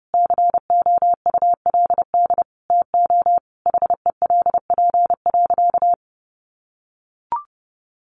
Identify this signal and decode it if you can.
Morse code, heard as a keyed tone, about 30 wpm, 700 Hz. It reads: COULD TO HELP.
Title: Sentences from Top 1100 Words 30wpm